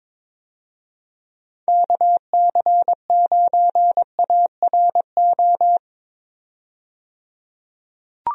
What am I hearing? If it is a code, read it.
Morse code, heard as a keyed tone, about 22 wpm, 700 Hz. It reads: KC9ARO
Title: USA Call Signs - 6 Characters - 22wpm